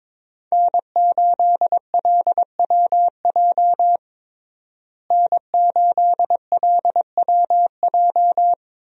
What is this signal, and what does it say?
Morse code, heard as a keyed tone, about 22 wpm, 700 Hz. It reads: N8LWJ N8LWJ
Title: USA Call Signs - 5 Characters - Repeated 3x - 22wpm